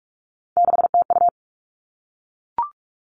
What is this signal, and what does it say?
Morse code, heard as a keyed tone, about 44 wpm, 700 Hz. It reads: 6TU
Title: All Cut Numbers - 3-Digit 45wpm